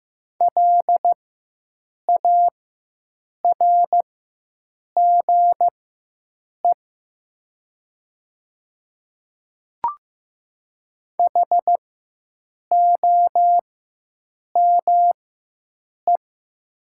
Morse code, heard as a keyed tone, about 15 wpm, spaced as for 7 wpm, 700 Hz. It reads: LARGE HOME
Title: 4x Character Spacing - Top 500 Words - 15wpm